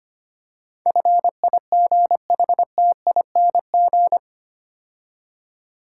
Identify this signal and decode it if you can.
Morse code, heard as a keyed tone, about 25 wpm, 700 Hz. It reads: FIGHTING
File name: 8 Letter Words - First 3 Letters Given - 25wpm